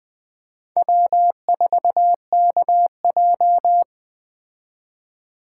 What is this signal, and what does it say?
Morse code, heard as a keyed tone, about 20 wpm, 700 Hz. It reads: W4KJ
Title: Speed Racing - USA 1x2 Call Signs 17wpm